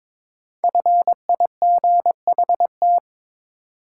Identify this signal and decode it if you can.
Morse code, heard as a keyed tone, about 22 wpm, 700 Hz. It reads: FIGHT
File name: 5 Letter Words - First 3 Letters Given - 22wpm